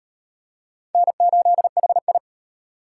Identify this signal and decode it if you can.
Morse code, heard as a keyed tone, about 37 wpm, 700 Hz. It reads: N8HI